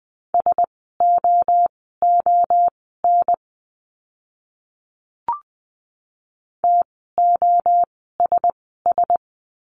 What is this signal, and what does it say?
Morse code, heard as a keyed tone, about 20 wpm, spaced as for 14 wpm, 700 Hz. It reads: SOON TOSS
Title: 054 Lesson 10f - Letter r - Words 20wpm